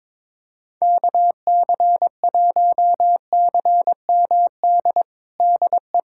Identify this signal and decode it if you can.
Morse code, heard as a keyed tone, about 22 wpm, 700 Hz. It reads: KC1CMD DE
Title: USA Callsign DE Callsign 22wpm